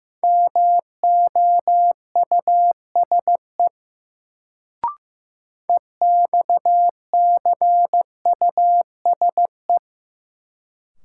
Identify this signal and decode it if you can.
Morse code, heard as a keyed tone, about 15 wpm, 700 Hz. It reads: MOUSE EXCUSE
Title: Sentences from Top 700 Words - New Word Drill - 15wpm